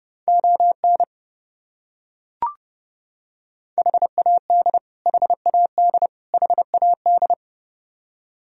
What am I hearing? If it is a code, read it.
Morse code, heard as a keyed tone, about 30 wpm, 700 Hz. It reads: ON HAD HAD HAD
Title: Top 100 Words Repeated 30wpm